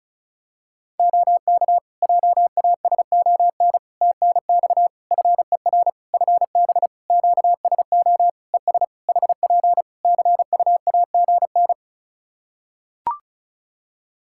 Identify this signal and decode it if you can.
Morse code, heard as a keyed tone, about 36 wpm, 700 Hz. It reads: OK JASON TNX FER FB QSO ES HP CUAGN
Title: Practice QSO - Exchange 4 - Ending - 35wpm